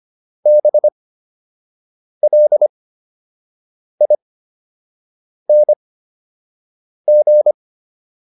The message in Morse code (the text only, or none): BLING